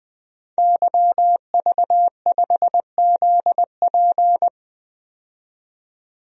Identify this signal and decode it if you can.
Morse code, heard as a keyed tone, about 20 wpm, 700 Hz. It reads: YV5ZP
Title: DXCC-100 Call Signs - 5 Characters - 20wpm